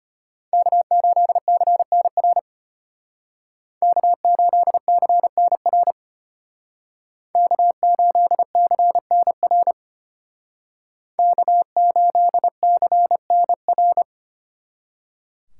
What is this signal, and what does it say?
Morse code, heard as a keyed tone, about 30 wpm, 700 Hz. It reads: K8CNR K8CNR K8CNR K8CNR
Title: Speed Racing - USA 1x3 Call Signs 25wpm